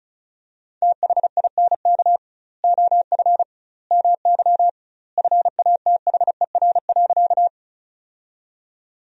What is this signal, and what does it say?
Morse code, heard as a keyed tone, about 34 wpm, 700 Hz. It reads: THINK OF MY FATHER.
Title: 2x Word-Spacing - Sentences from Top 300 Words - 35wpm